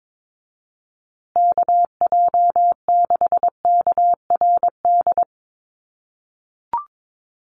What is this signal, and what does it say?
Morse code, heard as a keyed tone, about 22 wpm, 700 Hz. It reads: KJ6KRD
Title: USA Call Signs Repeated 2x 22wpm